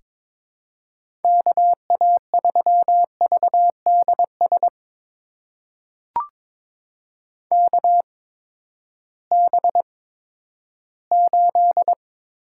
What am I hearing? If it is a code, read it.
Morse code, heard as a keyed tone, about 22 wpm, spaced as for 13 wpm, 700 Hz. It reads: KA3VDS KB8
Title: USA Call Signs - 6 Characters - 8x Character Spacing - 22wpm